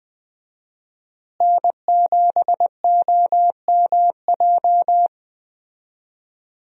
Morse code, heard as a keyed tone, about 20 wpm, 700 Hz. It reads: N7OMJ
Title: USA Call Signs - 5 Characters - Repeated 2x - 20wpm